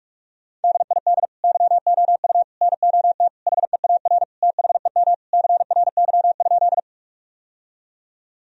Morse code, heard as a keyed tone, about 47 wpm, 700 Hz. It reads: DID YOU NOT HEAR THEM CRY?
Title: Sentences from Top 1000 Words - Review Part 2 - 45wpm